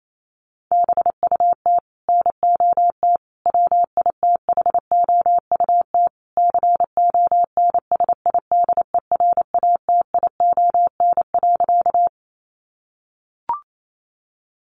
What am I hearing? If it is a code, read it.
Morse code, heard as a keyed tone, about 28 wpm, 700 Hz. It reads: BUT NOT WITHOUT CONSIDERATION.
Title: Sentences from Top 1500 Words 28wpm